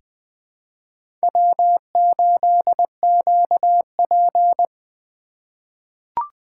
Morse code, heard as a keyed tone, about 21 wpm, 700 Hz. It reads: W8QP
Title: USA Call Signs - 4 Characters - 4x Character Spacing - 20wpm